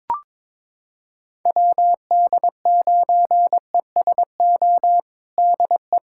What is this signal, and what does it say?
Morse code, heard as a keyed tone, about 22 wpm, 700 Hz. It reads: WD9ESO DE
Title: USA Callsign DE Callsign 22wpm